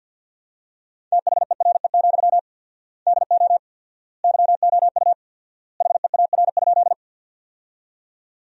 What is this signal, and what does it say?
Morse code, heard as a keyed tone, about 50 wpm, 700 Hz. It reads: THERE, DO YOU HEAR?
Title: 4x Word-Spacing - Sentences from Top 300 Words - 50wpm